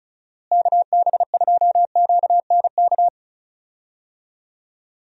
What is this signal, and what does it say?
Morse code, heard as a keyed tone, about 34 wpm, 700 Hz. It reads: KB2QNK
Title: USA Call Signs - 6 Characters - Repeated 3x - 35wpm